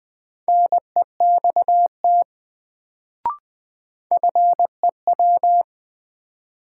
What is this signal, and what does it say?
Morse code, heard as a keyed tone, about 20 wpm, 700 Hz. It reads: NEXT FEW
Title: Sets of 2 Words - New Word Drill - 20wpm